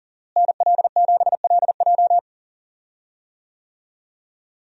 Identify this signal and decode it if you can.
Morse code, heard as a keyed tone, about 40 wpm, 700 Hz. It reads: NL7LJ
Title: USA Call Signs - 5 Characters - Repeated 2x - 40wpm